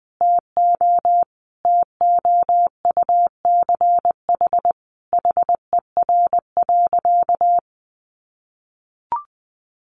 Morse code, heard as a keyed tone, about 20 wpm, 700 Hz. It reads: TO TOUCH HER.
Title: Sentences from Top 1200 Words 20wpm